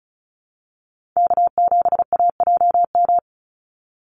Morse code, heard as a keyed tone, about 34 wpm, 700 Hz. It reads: K7AJM
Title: Call Signs - 35wpm